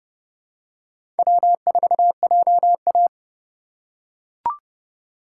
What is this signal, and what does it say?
Morse code, heard as a keyed tone, about 30 wpm, 700 Hz. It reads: W4JA
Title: USA Call Signs - 4 Characters - 4x Character Spacing - 30wpm